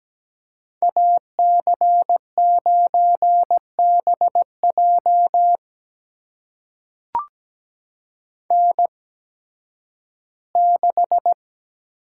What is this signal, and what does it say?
Morse code, heard as a keyed tone, about 17 wpm, spaced as for 11 wpm, 700 Hz. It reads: AC9BJ N6
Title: USA Call Signs - 5 Characters - 8x Character Spacing - 17wpm